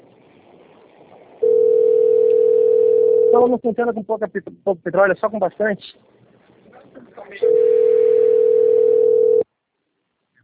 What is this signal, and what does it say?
Ringback tone (North America), 440 + 480 Hz, with speech.
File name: out- - - - - .9472